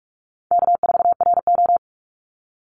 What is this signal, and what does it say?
Morse code, heard as a keyed tone, about 44 wpm, 700 Hz. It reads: K4RO
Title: Call Signs - 45wpm